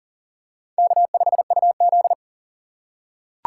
Morse code, heard as a keyed tone, about 40 wpm, 700 Hz. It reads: K5UZ
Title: USA Call Signs - 4 Characters - 3x Character Spacing - 40wpm